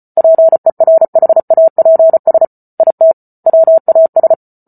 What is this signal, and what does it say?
Morse code, heard as a keyed tone, about 34 wpm, 650 Hz. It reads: PERHAPS IT WAS